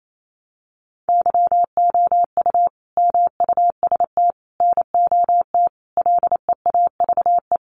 Morse code, heard as a keyed tone, about 28 wpm, 700 Hz. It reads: YOU MUST NOT LEAVE
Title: Sentences from Top 600 Words 28wpm